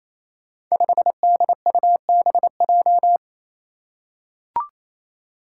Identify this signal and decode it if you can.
Morse code, heard as a keyed tone, about 29 wpm, 700 Hz. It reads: 5DUBJ
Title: Five Challenging Letter-Number Combinations 28wpm